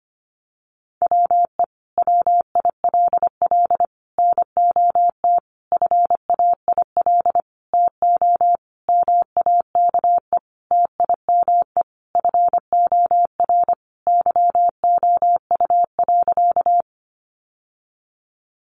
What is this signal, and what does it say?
Morse code, heard as a keyed tone, about 25 wpm, 700 Hz. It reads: WE WILL NOT FAIL TO MAKE TIME FOR YOU.
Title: Sentences from Top 1100 Words 25wpm